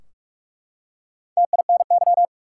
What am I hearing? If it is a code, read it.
Morse code, heard as a keyed tone, about 42 wpm, 700 Hz. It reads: TINY
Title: Four Letter Words - 45wpm